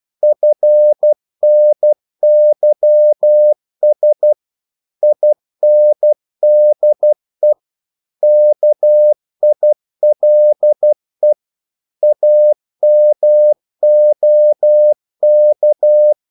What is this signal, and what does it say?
Morse code, heard as a keyed tone, about 12 wpm, 600 Hz. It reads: FNYS INDE KILE AMOK
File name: Korte ord DK 12wpm